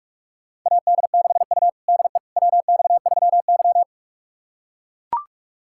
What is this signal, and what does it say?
Morse code, heard as a keyed tone, about 43 wpm, 700 Hz. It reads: AD6U DE WX3Y
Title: USA Callsign DE Callsign - 4 Characters - 45wpm